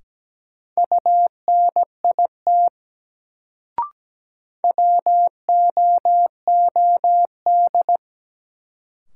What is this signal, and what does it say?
Morse code, heard as a keyed tone, about 17 wpm, 700 Hz. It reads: UNIT WOOD
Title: Sentences from Top 1000 Words - Review Part 2 - New Word Drill - 17wpm